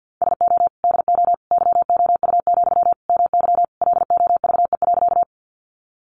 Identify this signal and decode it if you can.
Morse code, heard as a keyed tone, about 50 wpm, 700 Hz. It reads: SO DO YOU, MY LOVE.
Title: Sentences from Top 400 Words 50wpm